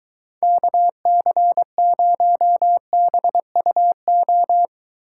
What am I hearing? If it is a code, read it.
Morse code, heard as a keyed tone, about 23 wpm, 700 Hz. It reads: KC0BUO